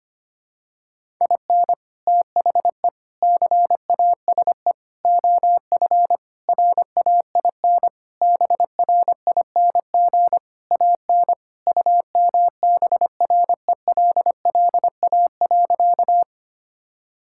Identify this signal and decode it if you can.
Morse code, heard as a keyed tone, about 25 wpm, 700 Hz. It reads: IN THE CASE OF RAIN BRING AN UMBRELLA.
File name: practice / 25wpm / Sentences with 3-Word Phrases - 25wpm